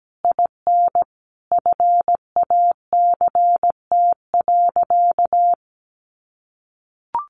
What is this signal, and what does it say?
Morse code, heard as a keyed tone, about 17 wpm, 700 Hz. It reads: IN FACT.
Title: Sentences from Top 1300 Words 17wpm